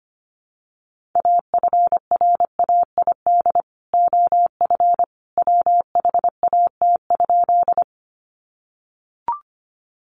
Morse code, heard as a keyed tone, about 25 wpm, 700 Hz. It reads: AFRAID OF WHAT?
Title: Sentences from Top 1500 Words 25wpm